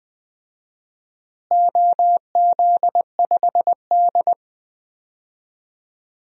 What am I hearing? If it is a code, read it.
Morse code, heard as a keyed tone, about 21 wpm, 700 Hz. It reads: OZ5D
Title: DXCC-100 Call Signs - 4 Characters - Repeated 2x - 20wpm